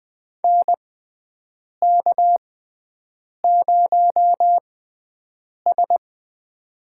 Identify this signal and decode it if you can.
Morse code, heard as a keyed tone, about 20 wpm, spaced as for 7 wpm, 700 Hz. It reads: NK0S
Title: USA Call Signs - 4 Characters - 6x Character Spacing - 20wpm